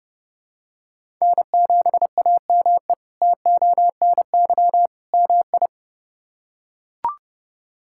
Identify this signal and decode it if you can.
Morse code, heard as a keyed tone, about 30 wpm, 700 Hz. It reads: N7AME TONY MI
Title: K1USN SST QSO Practice - 30wpm